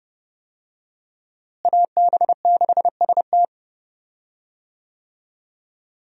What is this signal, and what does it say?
Morse code, heard as a keyed tone, about 32 wpm, 700 Hz. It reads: AB6ST
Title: USA Call Signs - 5 Characters - Repeated 3x - 30wpm